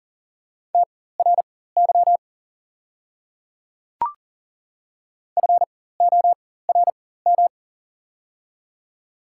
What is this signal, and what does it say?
Morse code, heard as a keyed tone, about 40 wpm, spaced as for 19 wpm, 690 Hz. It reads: TRY FORM
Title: 4x Character Spacing - Top 300 Words - 40wpm